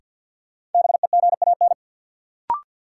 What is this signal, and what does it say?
Morse code, heard as a keyed tone, about 53 wpm, 700 Hz. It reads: BEGAN
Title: Sentences from Top 300 Words - New Word Drill - 50wpm